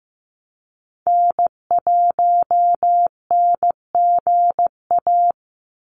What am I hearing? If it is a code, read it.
Morse code, heard as a keyed tone, about 15 wpm, 700 Hz. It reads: N1NGA
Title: USA Call Signs 15wpm